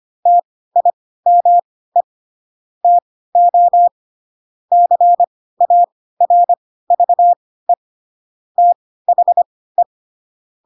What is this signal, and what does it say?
Morse code, heard as a keyed tone, about 25 wpm, spaced as for 16 wpm, 700 Hz. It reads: TIME TO CARVE THE
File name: Pinocchio / Pinocchio ch3 10 25 ik1hge